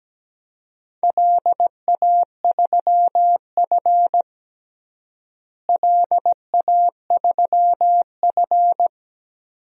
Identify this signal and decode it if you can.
Morse code, heard as a keyed tone, about 17 wpm, 700 Hz. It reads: LA3F LA3F
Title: DXCC-100 Call Signs - 4 Characters - Repeated 2x - 17wpm